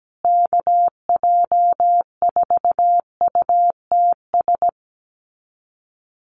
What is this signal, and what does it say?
Morse code, heard as a keyed tone, about 17 wpm, 700 Hz. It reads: KJ4UTS
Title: USA Call Signs Repeated 3x 17wpm